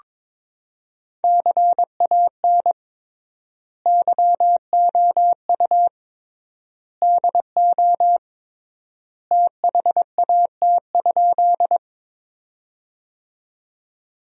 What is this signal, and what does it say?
Morse code, heard as a keyed tone, about 21 wpm, 700 Hz. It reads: CAN YOU DO THAT?
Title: 3x Word-Spacing - Sentences from Top 100 Words - 22wpm